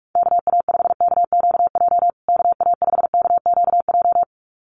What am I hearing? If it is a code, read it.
Morse code, heard as a keyed tone, about 44 wpm, 700 Hz. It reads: KA5KQJ KA5KQJ
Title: USA Call Signs Repeated 2x 45wpm